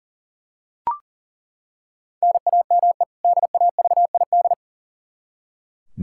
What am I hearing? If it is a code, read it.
Morse code, heard as a keyed tone, about 37 wpm, 700 Hz. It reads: NAME DAVID
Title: K1USN SST QSO Practice 40wpm